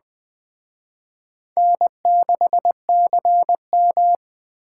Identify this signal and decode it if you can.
Morse code, heard as a keyed tone, about 20 wpm, 700 Hz. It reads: N6CM
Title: Speed Racing - USA 1x2 Call Signs 17wpm